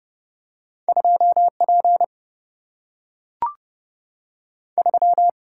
Morse code, heard as a keyed tone, about 30 wpm, 700 Hz. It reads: 2 P 3
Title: Single-Letter Single-Number 30wpm